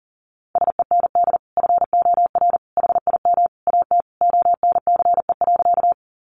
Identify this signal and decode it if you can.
Morse code, heard as a keyed tone, about 40 wpm, 700 Hz. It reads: SEND FOR HIM AT ONCE.
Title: Sentences from Top 1500 Words 40wpm